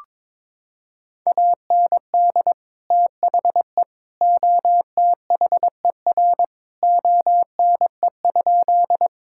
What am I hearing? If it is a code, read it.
Morse code, heard as a keyed tone, about 22 wpm, 700 Hz. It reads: AND THE OTHER ONE?
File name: practice / 22wpm / Sentences with 3-Word Phrases - 22wpm